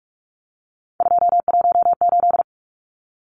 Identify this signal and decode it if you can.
Morse code, heard as a keyed tone, about 44 wpm, 700 Hz. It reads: 218
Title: Numbers - 3-Digit 45wpm